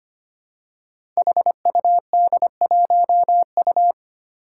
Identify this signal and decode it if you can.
Morse code, heard as a keyed tone, about 25 wpm, 700 Hz. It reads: HUD1U
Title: Five Challenging Letter-Number Combinations 25wpm